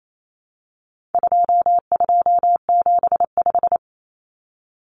Morse code, heard as a keyed tone, about 28 wpm, 700 Hz. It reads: 2275
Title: Common Cut Numbers - 4-Digit 28wpm